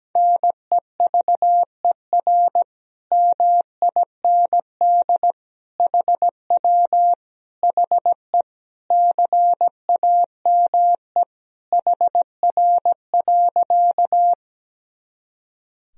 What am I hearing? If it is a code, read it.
Morse code, heard as a keyed tone, about 17 wpm, 700 Hz. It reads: NEVER MIND HW HE CAME HR.
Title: Sentences from Top 500 Words and Abbreviations 17wpm